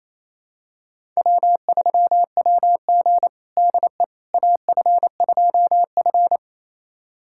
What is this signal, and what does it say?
Morse code, heard as a keyed tone, about 27 wpm, 700 Hz. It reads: W3WG DE AF2F
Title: USA Callsign DE Callsign - 4 Characters - 28wpm